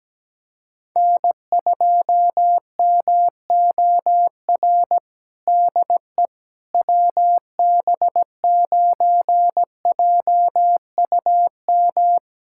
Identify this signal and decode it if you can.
Morse code, heard as a keyed tone, about 17 wpm, 700 Hz. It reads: N2MOR DE WB9JUM
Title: USA Callsign DE Callsign 17wpm